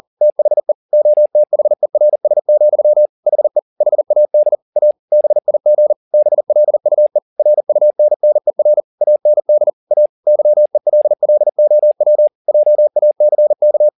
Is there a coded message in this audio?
Yes: Morse code from 0.2 to 13.9 s, heard as a keyed tone, about 40 wpm, 600 Hz. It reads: THE OTHERS, HE HAD A BIG BLUE RUNNER AND A YELLOW JACK